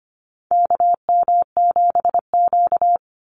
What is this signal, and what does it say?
Morse code, heard as a keyed tone, about 25 wpm, 700 Hz. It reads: KM7Q
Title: Call Signs Repeated 2x - 25wpm